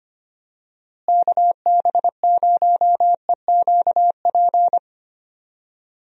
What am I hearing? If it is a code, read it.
Morse code, heard as a keyed tone, about 25 wpm, 700 Hz. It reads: KB0EQP
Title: USA Call Signs - 6 Characters - Repeated 2x - 25wpm